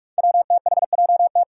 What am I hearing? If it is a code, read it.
Morse code, heard as a keyed tone, about 46 wpm, 700 Hz. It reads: WTHJT
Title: Five Challenging Letter-Number Combinations 45wpm